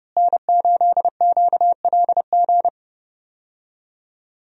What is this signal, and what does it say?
Morse code, heard as a keyed tone, about 29 wpm, 700 Hz. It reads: N8QLG
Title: USA Call Signs - 5 Characters - Repeated 2x - 30wpm